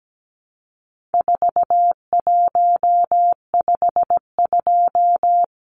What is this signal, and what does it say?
Morse code, heard as a keyed tone, about 17 wpm, 700 Hz. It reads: 4152
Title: Numbers - 4-Digit 17wpm